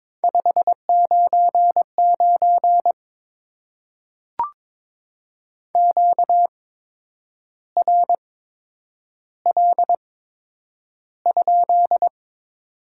Morse code, heard as a keyed tone, about 22 wpm, spaced as for 8 wpm, 700 Hz. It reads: 599 QRL?
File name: Top 90 QSO Elements - 8x Character Spacing - 22wpm